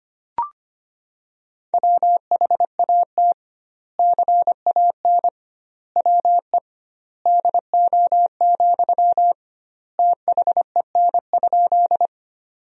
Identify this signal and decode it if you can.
Morse code, heard as a keyed tone, about 25 wpm, 700 Hz. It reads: WHAT CAN WE DO, THEN?
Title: 2x Word-Spacing - Sentences from Top 100 Words - 25wpm